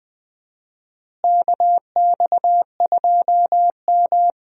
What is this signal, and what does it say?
Morse code, heard as a keyed tone, about 20 wpm, 700 Hz. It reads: KX2M
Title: USA Call Signs - 4 Characters - Repeated 3x - 20wpm